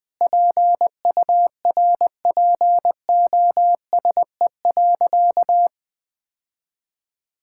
Morse code, heard as a keyed tone, about 20 wpm, 700 Hz. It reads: PURPOSE.
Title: Sentences from Top 1000 Words - Review Part 5 - 20wpm